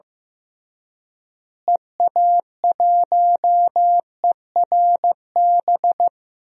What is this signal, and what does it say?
Morse code, heard as a keyed tone, about 15 wpm, 700 Hz. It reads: EA1ERB
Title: DXCC-100 Call Signs - 6 Characters - Repeated 3x - 15wpm